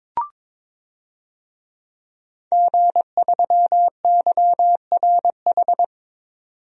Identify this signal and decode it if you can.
Morse code, heard as a keyed tone, about 22 wpm, 700 Hz. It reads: G3YRH
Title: DXCC-100 Call Signs - 5 Characters - 22wpm